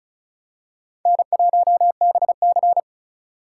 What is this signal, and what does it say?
Morse code, heard as a keyed tone, about 36 wpm, 700 Hz. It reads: N1BC
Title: USA Call Signs - 4 Characters - 3x Character Spacing - 35wpm